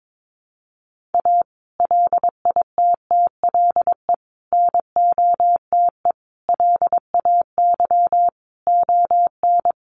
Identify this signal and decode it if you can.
Morse code, heard as a keyed tone, about 22 wpm, 700 Hz. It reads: A LITTLE NOTE LAY ON
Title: Sentences from Top 1500 Words 22wpm